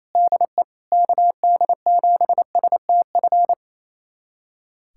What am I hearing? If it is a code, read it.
Morse code, heard as a keyed tone, about 28 wpm, 700 Hz. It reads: DE KD7STF
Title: USA Callsign DE Callsign 28wpm